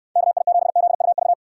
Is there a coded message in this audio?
Yes: Morse code from 0.2 to 1.3 s, heard as a keyed tone, about 69 wpm, 700 Hz. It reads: KI7ZUV